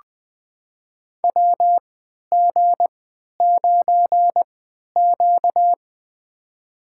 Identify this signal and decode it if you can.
Morse code, heard as a keyed tone, about 20 wpm, spaced as for 11 wpm, 700 Hz. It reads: WG9Q
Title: USA Call Signs - 4 Characters - 3x Character Spacing - 20wpm